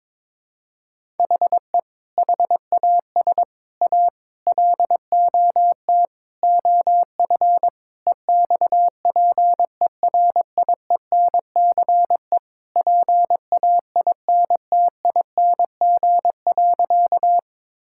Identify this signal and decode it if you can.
Morse code, heard as a keyed tone, about 22 wpm, 700 Hz. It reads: HE HAS A LOT OF EXPERIENCE PAINTING.
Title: Sentences from Top 1000 Words - Review Part 6 - 22wpm